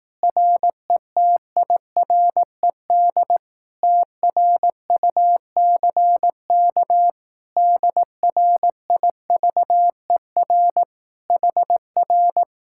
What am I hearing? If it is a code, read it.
Morse code, heard as a keyed tone, about 18 wpm, 700 Hz. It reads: RETIRED TRUCK DRIVER HR
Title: Speed-Racing Common QSO Phrases Encore - 15wpm